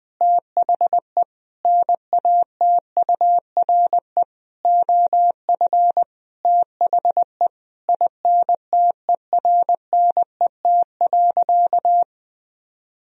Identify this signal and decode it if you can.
Morse code, heard as a keyed tone, about 21 wpm, 700 Hz. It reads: THE NATURE OF THE INTERNET.
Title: Sentences from Top 1000 Words - Review Part 4 - 20wpm